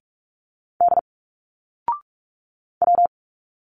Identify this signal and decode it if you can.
Morse code, heard as a keyed tone, about 46 wpm, 700 Hz. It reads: D W
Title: Single Letters 45wpm